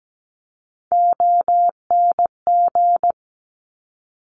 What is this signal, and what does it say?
Morse code, heard as a keyed tone, about 17 wpm, 690 Hz. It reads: ONG
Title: Top Three Letter Combinations in Words 17wpm